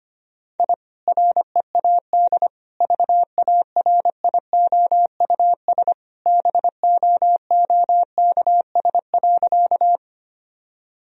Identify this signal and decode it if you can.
Morse code, heard as a keyed tone, about 25 wpm, 700 Hz. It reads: I READ VARIOUS BOOKS.
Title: Sentences from Top 1000 Words - Review Part 4 - 25wpm